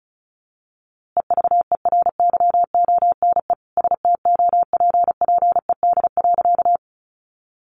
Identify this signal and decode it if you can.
Morse code, heard as a keyed tone, about 34 wpm, 700 Hz. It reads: EVERYONE STOPPED.
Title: A Sound Of Thunder 35wpm